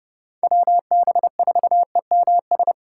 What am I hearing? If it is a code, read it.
Morse code, heard as a keyed tone, about 30 wpm, 700 Hz. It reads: WB4EMS
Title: USA Call Signs - 2x Character Spacing - 30wpm